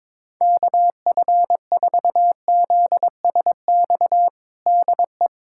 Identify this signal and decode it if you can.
Morse code, heard as a keyed tone, about 22 wpm, 700 Hz. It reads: KF4ZSX DE
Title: USA Callsign DE Callsign 22wpm